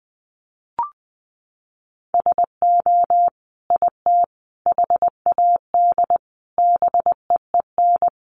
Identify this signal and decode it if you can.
Morse code, heard as a keyed tone, about 20 wpm, 700 Hz. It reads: SO IT HAD BEEN
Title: Sentences from Top 1200 Words 20wpm